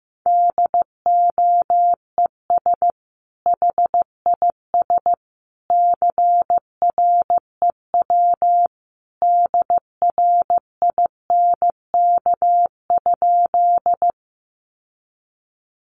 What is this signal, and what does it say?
Morse code, heard as a keyed tone, about 15 wpm, 700 Hz. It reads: DOES HIS CREW DRINK?
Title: Sentences from Top 1500 Words 15wpm